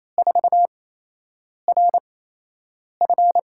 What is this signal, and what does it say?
Morse code, heard as a keyed tone, about 28 wpm, spaced as for 8 wpm, 700 Hz. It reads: 4RF
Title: USA Call Signs - 5 Characters - 8x Character Spacing - 28wpm